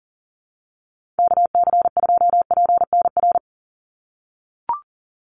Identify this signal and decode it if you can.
Morse code, heard as a keyed tone, about 40 wpm, 700 Hz. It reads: KC2PNR
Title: USA Call Signs 40wpm